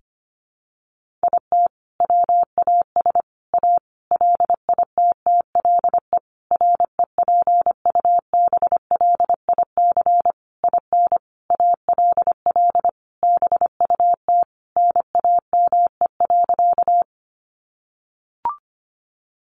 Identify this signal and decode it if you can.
Morse code, heard as a keyed tone, about 25 wpm, 700 Hz. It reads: IT WAS A LITTLE REPUBLIC IN ALL BUT NAME.
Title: Sentences from Top 1500 Words 25wpm